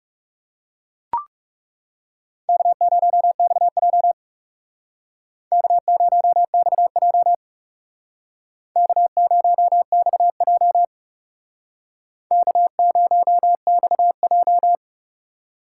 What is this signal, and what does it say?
Morse code, heard as a keyed tone, about 35 wpm, 700 Hz. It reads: K0XJ K0XJ K0XJ K0XJ
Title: Speed Racing - USA 1x2 Call Signs 30wpm